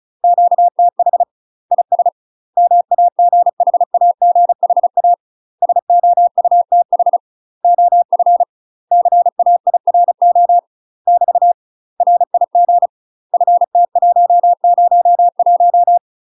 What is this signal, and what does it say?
Morse code, heard as a keyed tone, about 36 wpm, 700 Hz. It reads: QTH IS MAGHAGHA SOUTH OF CAIRO <BT> RIG FT101
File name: UK MASTER QSOs TRAINING / 35WPM / UK MASTER QSOs 0035